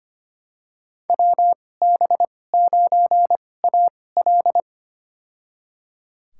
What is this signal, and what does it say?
Morse code, heard as a keyed tone, about 25 wpm, spaced as for 18 wpm, 700 Hz. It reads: WB9AL
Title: USA Call Signs - 5 Characters - 2x Character Spacing - 25wpm